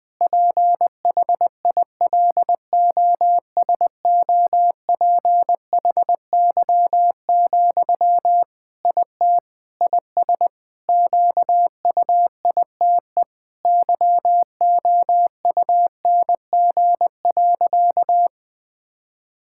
Morse code, heard as a keyed tone, about 20 wpm, 700 Hz. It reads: PHILOSOPHY, IT IS QUITE YOUNG.
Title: Sentences from Top 1000 Words - Review Part 6 - 20wpm